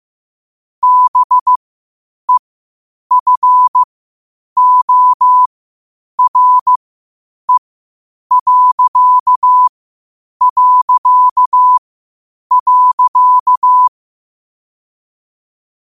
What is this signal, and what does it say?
Morse code, heard as a keyed tone, about 15 wpm, spaced as for 8 wpm, 1000 Hz. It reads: BEFORE...
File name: qotdfiles / 8wpm